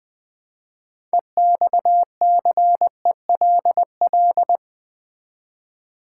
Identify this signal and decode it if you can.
Morse code, heard as a keyed tone, about 20 wpm, 700 Hz. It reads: EXCELL
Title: Build A Word - 3 Letters At A Time - 9-Letter Words - 20wpm